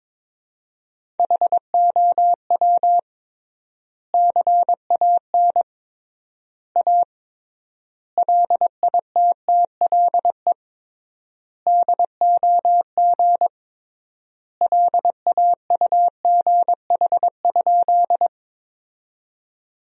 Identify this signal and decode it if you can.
Morse code, heard as a keyed tone, about 22 wpm, 700 Hz. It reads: HOW CAN A LITTLE DOG LAUGH?
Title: 3x Word-Spacing - Sentences from Top 500 Words - 22wpm